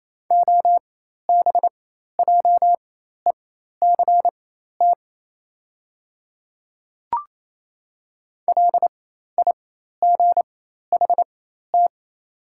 Morse code, heard as a keyed tone, about 27 wpm, spaced as for 13 wpm, 700 Hz. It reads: OBJECT LIGHT
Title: 4x Character Spacing - Top 500 Words - 28wpm